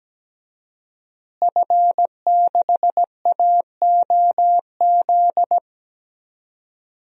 Morse code, heard as a keyed tone, about 17 wpm, 700 Hz. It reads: F6AOZ
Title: DXCC-100 Call Signs - 5 Characters - Repeated 3x - 17wpm